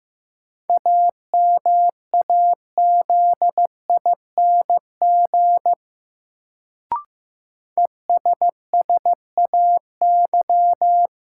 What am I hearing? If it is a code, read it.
Morse code, heard as a keyed tone, about 15 wpm, 700 Hz. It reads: AMAZING ESSAY
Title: Sentences from Top 1500 Words - New Word Drill - 15wpm